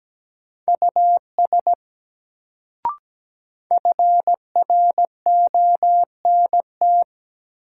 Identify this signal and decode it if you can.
Morse code, heard as a keyed tone, about 17 wpm, 700 Hz. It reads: US FRONT
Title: Sets of 2 Words - New Word Drill - 17wpm